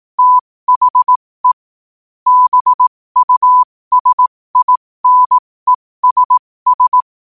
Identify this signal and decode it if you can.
Morse code, heard as a keyed tone, about 18 wpm, spaced as for 15 wpm, 1000 Hz. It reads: THE BUSINESS
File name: qotdfiles / 15wpm